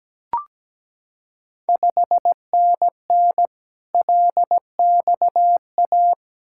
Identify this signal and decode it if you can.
Morse code, heard as a keyed tone, about 17 wpm, 690 Hz. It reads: 5NN LXA
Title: Ontario QSO Party Exchange 17wpm